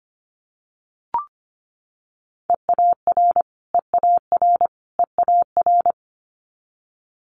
Morse code, heard as a keyed tone, about 25 wpm, 700 Hz. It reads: EAR EAR EAR
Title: Multi-letter Phonemes Repeated 3x 25wpm